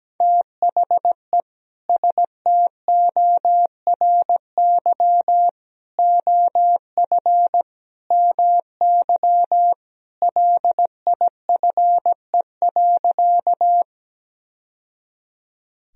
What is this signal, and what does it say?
Morse code, heard as a keyed tone, about 17 wpm, 700 Hz. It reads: THE STORY OF MY LIFE.
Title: Sentences from Top 1000 Words - Review Part 1 - 17wpm